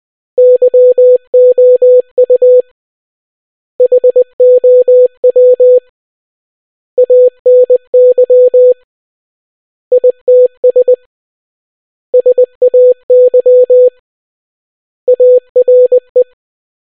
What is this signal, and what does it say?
Morse code, heard as a keyed tone, about 20 wpm, 500 Hz. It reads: YOU HOW ANY ITS SAY ARE